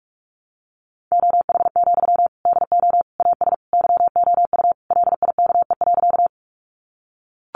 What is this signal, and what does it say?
Morse code, heard as a keyed tone, about 44 wpm, 700 Hz. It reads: OH, DO AS YOU LIKE.
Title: Sentences from Top 500 Words 45wpm